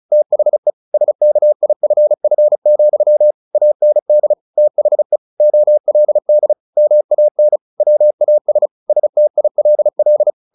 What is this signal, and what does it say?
Morse code, heard as a keyed tone, about 34 wpm, 600 Hz. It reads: THE SKIFF, AND THE OLD MAN WAS STILL
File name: the old man and the sea / the old man and the sea 1 35wpm